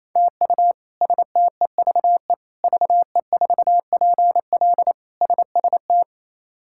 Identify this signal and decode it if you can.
Morse code, heard as a keyed tone, about 28 wpm, 700 Hz. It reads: TU STEVE VE4PL SST
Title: K1USN SST QSO Practice - 28wpm